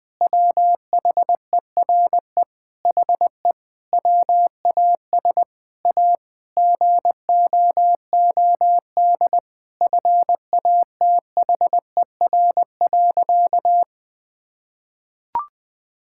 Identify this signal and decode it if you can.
Morse code, heard as a keyed tone, about 19 wpm, 700 Hz. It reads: WHERE HE WAS A GOOD FATHER.
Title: Sentences from Top 1000 Words - Review Part 5 - 20wpm